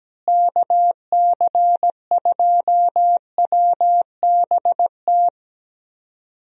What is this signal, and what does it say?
Morse code, heard as a keyed tone, about 17 wpm, 700 Hz. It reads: KC2WBT
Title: USA Call Signs - 6 Characters - Repeated 3x - 17wpm